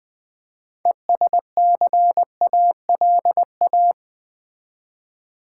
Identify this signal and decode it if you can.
Morse code, heard as a keyed tone, about 20 wpm, 700 Hz. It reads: ESCALA
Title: Build A Word - 2 Letters At A Time - 8-Letter Words - 20wpm